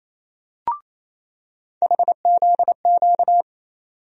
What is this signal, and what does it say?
Morse code, heard as a keyed tone, about 28 wpm, 700 Hz. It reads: HZQ
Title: USA Call Sign Postfixes - 3 Letters 28wpm